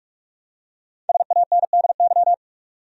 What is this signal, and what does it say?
Morse code, heard as a keyed tone, about 46 wpm, 700 Hz. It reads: SANDY
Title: USA Common Names - Max 5 Letters - 45wpm